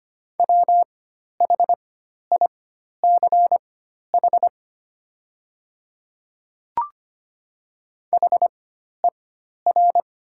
Morse code, heard as a keyed tone, about 25 wpm, spaced as for 12 wpm, 690 Hz. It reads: WHICH HER
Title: 4x Character Spacing - Top 200 Words - 25wpm